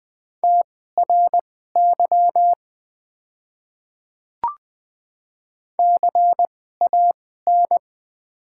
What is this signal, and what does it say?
Morse code, heard as a keyed tone, about 20 wpm, spaced as for 14 wpm, 700 Hz. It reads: TRY CAN
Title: 2x Character Spacing - Top 300 Words - 20wpm